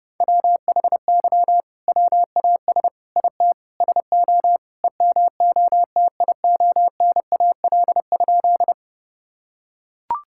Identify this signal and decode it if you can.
Morse code, heard as a keyed tone, about 30 wpm, 700 Hz. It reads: WHY WAS IT SO EMOTIONAL?
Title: Sentences from Top 1000 Words - Review Part 5 - 30wpm